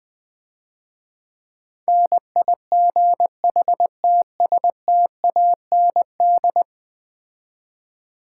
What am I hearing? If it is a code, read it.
Morse code, heard as a keyed tone, about 20 wpm, 700 Hz. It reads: NIGHTSTAND
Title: Easy Words - Speed-Racing with Hints - 20wpm